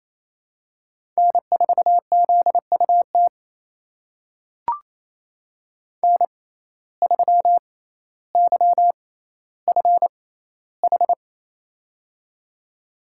Morse code, heard as a keyed tone, about 28 wpm, spaced as for 14 wpm, 700 Hz. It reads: N4ZUT N3YFH
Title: USA Call Signs - 5 Characters - 6x Character Spacing - 28wpm